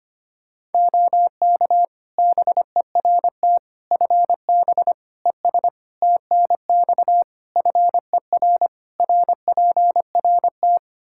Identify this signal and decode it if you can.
Morse code, heard as a keyed tone, about 25 wpm, 700 Hz. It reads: OK BERT FB ES TNX FER RPRT
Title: Practice QSO - Exchange 2 - Info - 25wpm